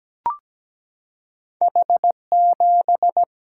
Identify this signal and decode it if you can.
Morse code, heard as a keyed tone, about 17 wpm, 700 Hz. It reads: H7
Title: Single-Letter Single-Number 17wpm